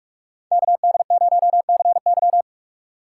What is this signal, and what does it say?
Morse code, heard as a keyed tone, about 44 wpm, 700 Hz. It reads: KD0CY